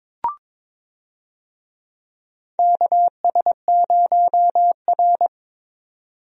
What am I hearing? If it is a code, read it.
Morse code, heard as a keyed tone, about 22 wpm, 700 Hz. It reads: KS0R
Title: USA Call Signs - 4 Characters - 22wpm